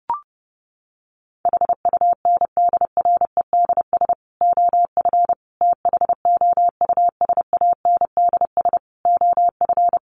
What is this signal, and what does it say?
Morse code, heard as a keyed tone, about 30 wpm, 700 Hz. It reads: HUNDREDS OF THOUSANDS OF
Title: Sets of 5 Words Encore 30wpm